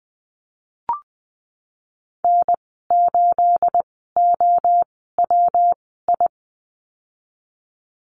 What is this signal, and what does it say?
Morse code, heard as a keyed tone, about 20 wpm, spaced as for 15 wpm, 700 Hz. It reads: N8OWI
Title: 238 Lesson 38g - Letter q - Call Signs 20wpm